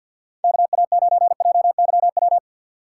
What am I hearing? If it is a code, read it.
Morse code, heard as a keyed tone, about 53 wpm, 700 Hz. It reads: KA9JYW
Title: USA Call Signs - 6 Characters - 4x Character Spacing - 50wpm